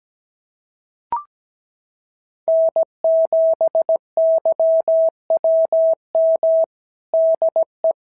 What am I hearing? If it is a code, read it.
Morse code, heard as a keyed tone, about 17 wpm, 650 Hz. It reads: N7YWM DE
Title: SOTA QSO Practice 17wpm